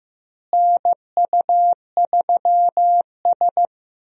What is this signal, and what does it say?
Morse code, heard as a keyed tone, about 15 wpm, 700 Hz. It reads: NU3S
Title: USA Call Signs - 4 Characters - Repeated 3x - 15wpm